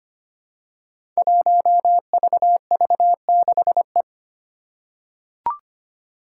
Morse code, heard as a keyed tone, about 25 wpm, 700 Hz. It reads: 1VV6E
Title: Five Challenging Letter-Number Combinations 25wpm